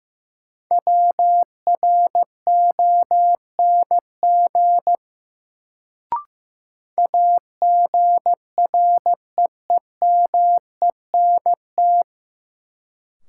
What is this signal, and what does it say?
Morse code, heard as a keyed tone, about 15 wpm, 700 Hz. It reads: WRONG AGREEMENT
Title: Sets of 5 Words Encore - New Word Drill - 15wpm